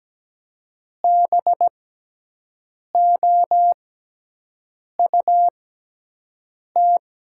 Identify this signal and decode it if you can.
Morse code, heard as a keyed tone, about 17 wpm, spaced as for 6 wpm, 700 Hz. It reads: BOUT
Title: 6x Character Spacing - Top 100 Words - 17wpm